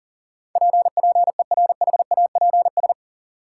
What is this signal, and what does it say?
Morse code, heard as a keyed tone, about 40 wpm, 700 Hz. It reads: P PERHAPS
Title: Sentences from Top 1100 Words - New Word Drill Alt - 40wpm